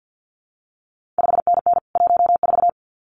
Morse code, heard as a keyed tone, about 50 wpm, 700 Hz. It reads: 5NN 14
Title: CQ WW DX Contest Exchange - 50wpm